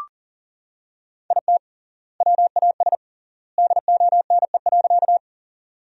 Morse code, heard as a keyed tone, about 40 wpm, 700 Hz. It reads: IT WAS DONE.
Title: 3x Word-Spacing - Sentences from Top 500 Words - 40wpm